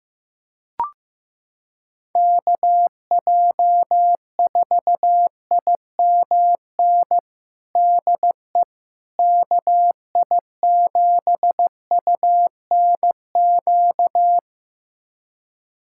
Morse code, heard as a keyed tone, about 15 wpm, 700 Hz. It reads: KJ4IMN DE KI7UNQ
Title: USA Callsign DE Callsign - 6 Characters - 15wpm